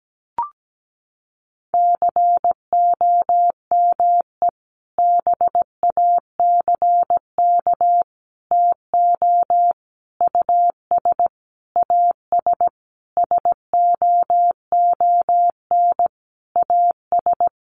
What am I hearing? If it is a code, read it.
Morse code, heard as a keyed tone, about 17 wpm, 700 Hz. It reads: COME BACK TO US AS SOON AS
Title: Sentences from Top 500 Words 17wpm